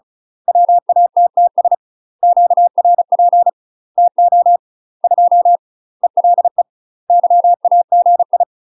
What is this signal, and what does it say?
Morse code, heard as a keyed tone, about 36 wpm, 700 Hz. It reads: WATTS QRP TO 2 ELE YAGI
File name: UK MASTER QSOs TRAINING / 35WPM / UK MASTER QSOs 0022